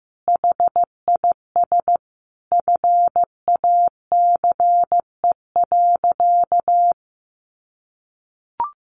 Morse code, heard as a keyed tone, about 15 wpm, 700 Hz. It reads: HIS FACE.
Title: Sentences from Top 1100 Words 15wpm